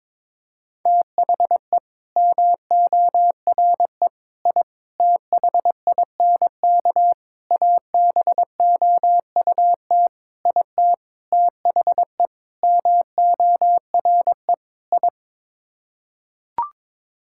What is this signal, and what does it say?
Morse code, heard as a keyed tone, about 22 wpm, 700 Hz. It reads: THE MORE I THINK ABOUT IT THE MORE I
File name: Sets of 9 Words - 22wpm